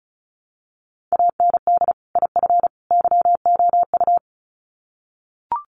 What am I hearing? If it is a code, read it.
Morse code, heard as a keyed tone, about 34 wpm, 700 Hz. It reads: AND IF YOU
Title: Sets of 3 Words 35wpm